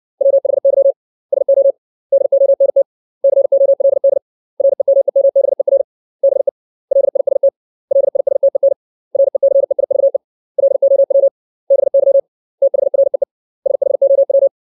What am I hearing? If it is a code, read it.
Morse code, heard as a keyed tone, about 60 wpm, 550 Hz. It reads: WHY SO DONT COLD REMEMBER BE LIST LISTEN RECEIVE LOW BY THREE SHOW
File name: 632 words / 632 Words 60WPM